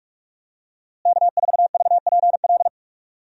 Morse code, heard as a keyed tone, about 46 wpm, 700 Hz. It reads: K4VPL
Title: USA Call Signs - 5 Characters - Repeated 3x - 45wpm